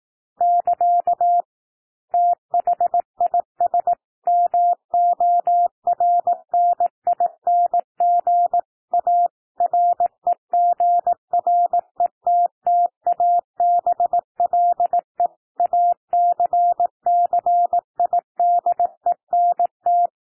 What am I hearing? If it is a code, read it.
Morse code, heard as a keyed tone, about 18 wpm, 700 Hz. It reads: <KA> THIS MORNING A REGRETTABLE ACCIDENT
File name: TEXT PASSAGE 18 wpm